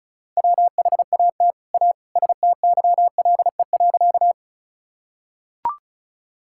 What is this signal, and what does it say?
Morse code, heard as a keyed tone, about 36 wpm, 700 Hz. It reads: WHAT A STYLE.
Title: Sentences from Top 1000 Words - Review Part 5 - 35wpm